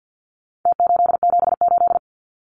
Morse code, heard as a keyed tone, about 48 wpm, 700 Hz. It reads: T878
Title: Numbers - Contest Style - 4-Digit 50wpm